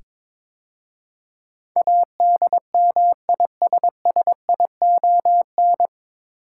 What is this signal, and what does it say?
Morse code, heard as a keyed tone, about 22 wpm, 700 Hz. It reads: ADMISSION
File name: 9 Letter Words - First 3 Letters Given - 22wpm